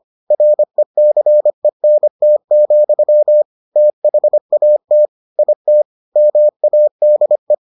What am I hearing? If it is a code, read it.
Morse code, heard as a keyed tone, about 25 wpm, 600 Hz. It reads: RECENT, THAT IT MADE